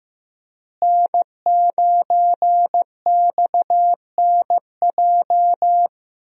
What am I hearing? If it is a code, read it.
Morse code, heard as a keyed tone, about 14 wpm, 700 Hz. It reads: N9XNJ
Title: USA Call Signs - 5 Characters - Repeated 3x - 15wpm